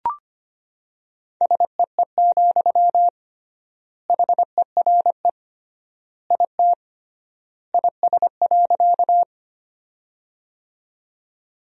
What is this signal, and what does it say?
Morse code, heard as a keyed tone, about 25 wpm, 700 Hz. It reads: SEE, HERE IT IS.
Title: 3x Word-Spacing - Sentences from Top 200 Words - 25wpm